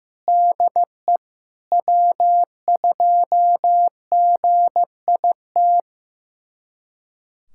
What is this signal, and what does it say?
Morse code, heard as a keyed tone, about 15 wpm, 700 Hz. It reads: DE W2GIT
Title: USA Callsign DE Callsign - 5 Characters - 15wpm